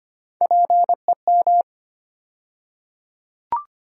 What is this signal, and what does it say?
Morse code, heard as a keyed tone, about 25 wpm, 700 Hz. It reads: PEM
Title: USA Call Sign Postfixes - 3 Letters 25wpm